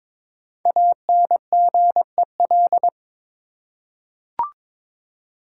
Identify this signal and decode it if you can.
Morse code, heard as a keyed tone, about 22 wpm, 700 Hz. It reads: ANGEL
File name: 5 Letter Words - First 3 Letters Given - 22wpm